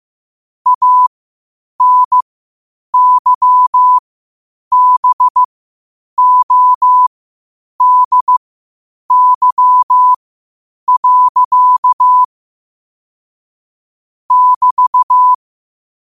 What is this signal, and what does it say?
Morse code, heard as a keyed tone, about 15 wpm, spaced as for 8 wpm, 1000 Hz. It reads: ANYBODY. <BT>
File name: qotdfiles / 8wpm